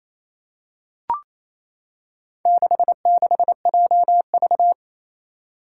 Morse code, heard as a keyed tone, about 28 wpm, 700 Hz. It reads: 66JV
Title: Four Challenging Letter-Number Combinations 28wpm